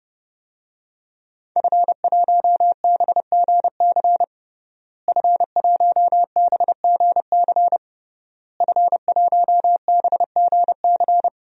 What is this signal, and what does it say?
Morse code, heard as a keyed tone, about 30 wpm, 700 Hz. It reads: F1BGC F1BGC F1BGC
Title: DXCC-100 Call Signs - 5 Characters - Repeated 3x - 30wpm